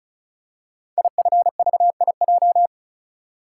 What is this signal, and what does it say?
Morse code, heard as a keyed tone, about 36 wpm, 700 Hz. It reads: IFVIJ
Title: Code Groups - 5 Letters - 35wpm